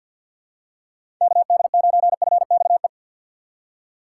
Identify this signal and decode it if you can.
Morse code, heard as a keyed tone, about 52 wpm, 700 Hz. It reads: KD9FXE
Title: USA Call Signs - 3x Character Spacing - 50wpm